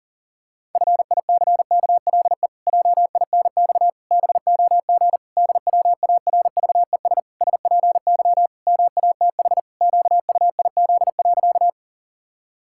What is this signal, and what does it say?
Morse code, heard as a keyed tone, about 40 wpm, 700 Hz. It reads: FICKLE JINX BOG DWARVES SPY MATH QUIZ.
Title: Pangram Challenge v2 - 40wpm